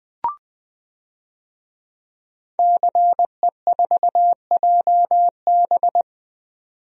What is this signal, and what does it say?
Morse code, heard as a keyed tone, about 21 wpm, 700 Hz. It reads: CE4JB
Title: DXCC-100 Call Signs - 5 Characters - 20wpm